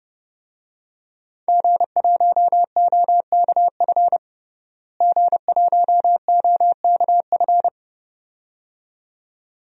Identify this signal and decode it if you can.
Morse code, heard as a keyed tone, about 32 wpm, 700 Hz. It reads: G1OKF G1OKF
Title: DXCC-100 Call Signs - 5 Characters - Repeated 2x - 30wpm